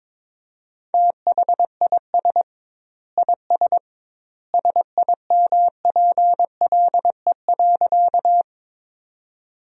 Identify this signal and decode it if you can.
Morse code, heard as a keyed tone, about 22 wpm, 700 Hz. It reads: THIS IS SIMPLE.
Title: 2x Word-Spacing - Sentences from Top 400 Words - 22wpm